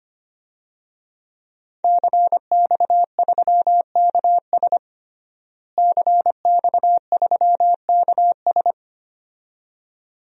Morse code, heard as a keyed tone, about 25 wpm, 700 Hz. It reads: CX3KS CX3KS
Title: DXCC-100 Call Signs - 5 Characters - Repeated 2x - 25wpm